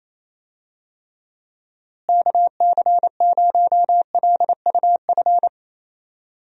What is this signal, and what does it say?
Morse code, heard as a keyed tone, about 27 wpm, 700 Hz. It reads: KC0LUF
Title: USA Call Signs - 6 Characters - Repeated 2x - 28wpm